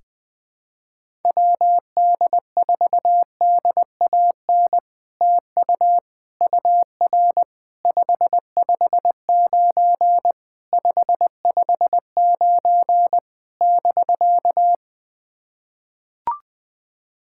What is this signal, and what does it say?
Morse code, heard as a keyed tone, about 21 wpm, 700 Hz. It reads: WD4DAN TU UR 559 559 <BK>
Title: POTA QSO Practice - 20wpm